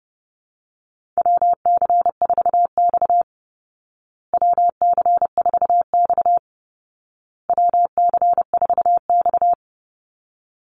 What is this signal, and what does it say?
Morse code, heard as a keyed tone, about 30 wpm, 700 Hz. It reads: WC4X WC4X WC4X
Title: Call Signs Repeated 2x - 30wpm